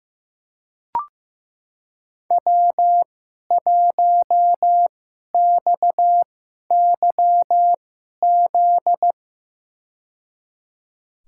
Morse code, heard as a keyed tone, about 15 wpm, spaced as for 11 wpm, 700 Hz. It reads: W1XYZ
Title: USA Call Signs - 5 Characters - 2x Character Spacing - 15wpm